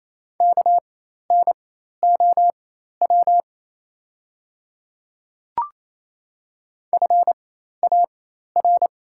Morse code, heard as a keyed tone, about 28 wpm, spaced as for 13 wpm, 700 Hz. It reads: KNOW FAR
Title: 4x Character Spacing - Top 300 Words - 28wpm